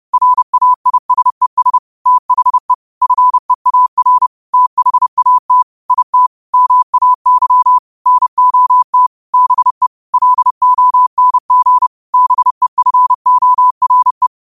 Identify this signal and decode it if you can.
Morse code, heard as a keyed tone, about 30 wpm, 1000 Hz. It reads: RAISES THE FEAR THAT IT MAY NOT BE LONG BEFORE
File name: qotdfiles / 30wpm